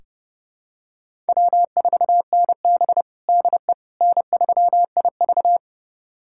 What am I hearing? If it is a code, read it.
Morse code, heard as a keyed tone, about 30 wpm, 700 Hz. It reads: W4NB DE N3IV
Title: USA Callsign DE Callsign - 4 Characters - 30wpm